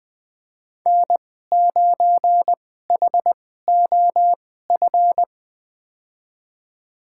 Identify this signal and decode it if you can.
Morse code, heard as a keyed tone, about 20 wpm, spaced as for 14 wpm, 700 Hz. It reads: N9HOF
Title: USA Call Signs - 5 Characters - 2x Character Spacing - 20wpm